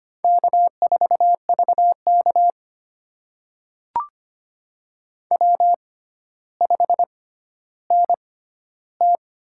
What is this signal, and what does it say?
Morse code, heard as a keyed tone, about 25 wpm, spaced as for 13 wpm, 700 Hz. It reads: K4VK W5NT
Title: USA Call Signs - 4 Characters - 6x Character Spacing - 25wpm